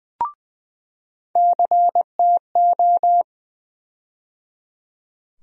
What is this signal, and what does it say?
Morse code, heard as a keyed tone, about 20 wpm, 700 Hz. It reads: CTO
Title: USA Call Sign Postfixes - 3 Letters 20wpm